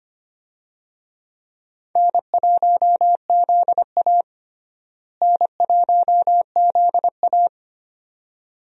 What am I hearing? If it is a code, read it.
Morse code, heard as a keyed tone, about 25 wpm, 700 Hz. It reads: N1ZA N1ZA
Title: USA Call Signs - 4 Characters - Repeated 2x - 25wpm